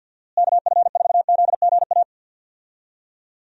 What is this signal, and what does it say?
Morse code, heard as a keyed tone, about 50 wpm, 700 Hz. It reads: KF4ZGA
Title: USA Call Signs - 6 Characters - Repeated 3x - 50wpm